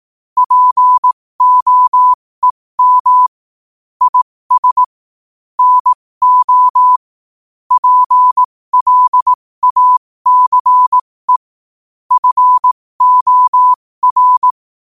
Morse code, heard as a keyed tone, about 18 wpm, spaced as for 15 wpm, 1000 Hz. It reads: POEM IS NO PLACE FOR
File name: qotdfiles / 15wpm